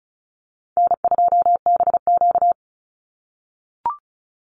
Morse code, heard as a keyed tone, about 34 wpm, 700 Hz. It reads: N2BQ
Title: USA Call Signs 35wpm